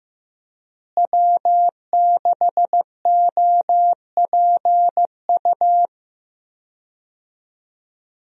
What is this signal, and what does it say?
Morse code, heard as a keyed tone, about 15 wpm, 700 Hz. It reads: W6OPU
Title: USA Call Signs - 5 Characters - Repeated 3x - 15wpm